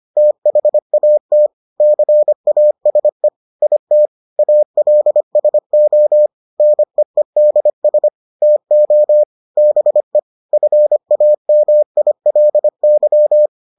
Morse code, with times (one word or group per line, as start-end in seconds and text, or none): 0.2-1.5 THAT
1.8-3.3 CASE
3.6-4.1 IT
4.4-6.3 ALSO
6.6-8.1 NEEDS
8.4-9.2 TO
9.6-10.2 BE
10.5-13.5 FAMILY